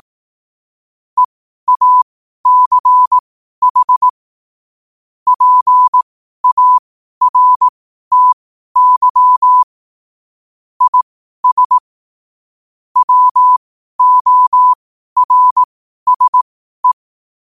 Morse code, heard as a keyed tone, about 18 wpm, spaced as for 12 wpm, 1000 Hz. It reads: EACH PARTY IS WORSE
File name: qotdfiles / 12wpm